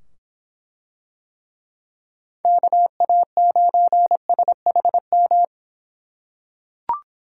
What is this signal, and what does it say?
Morse code, heard as a keyed tone, about 24 wpm, 700 Hz. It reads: KA9SHM